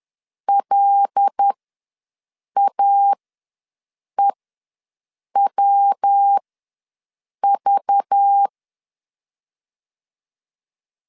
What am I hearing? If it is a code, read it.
Morse code, heard as a keyed tone, about 10 wpm, spaced as for 6 wpm, 800 Hz. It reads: LAEWV
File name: LEZIONI / 5 LBVC / 5 LBV ws 10wpm 3645